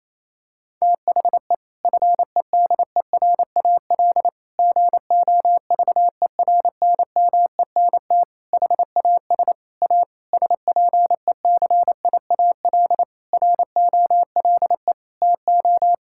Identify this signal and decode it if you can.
Morse code, heard as a keyed tone, about 27 wpm, 700 Hz. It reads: THE FEDERAL GOVERNMENT HAS A SPECIAL ROLE TO
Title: Sentences from Top 1000 Words - Review Part 6 - 28wpm